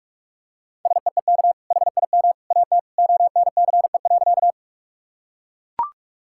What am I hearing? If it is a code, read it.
Morse code, heard as a keyed tone, about 41 wpm, 700 Hz. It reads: SEEK HIM AT ONCE.
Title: Sentences from Top 1600 Words 45wpm